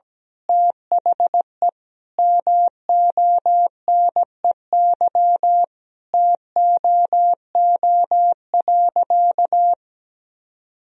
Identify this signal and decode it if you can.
Morse code, heard as a keyed tone, about 17 wpm, 700 Hz. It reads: THE MONEY TOO.
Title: Sentences from Top 1000 Words - Review Part 4 - 17wpm